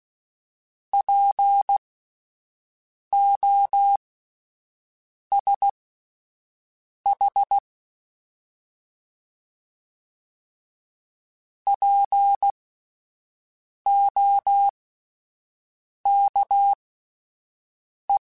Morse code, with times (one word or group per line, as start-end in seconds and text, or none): 0.9-7.6 POSH
11.7-18.2 POKE